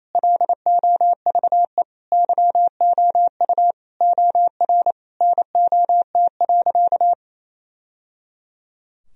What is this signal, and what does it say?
Morse code, heard as a keyed tone, about 28 wpm, 700 Hz. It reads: LOVE YOU OR NOT.
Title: Sentences from Top 1000 Words - Review Part 3 - 28wpm